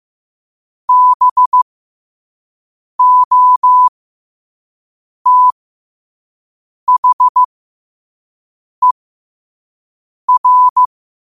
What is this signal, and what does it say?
Morse code, heard as a keyed tone, about 15 wpm, spaced as for 5 wpm, 1000 Hz. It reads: BOTHER
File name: qotdfiles / 5wpm